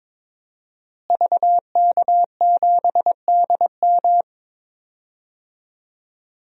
Morse code, heard as a keyed tone, about 22 wpm, 700 Hz. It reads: VK7DM
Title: DXCC-100 Call Signs - 5 Characters - Repeated 2x - 22wpm